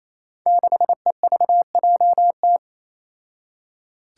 Morse code, heard as a keyed tone, about 28 wpm, 700 Hz. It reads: 6EVJT
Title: Five Challenging Letter-Number Combinations 28wpm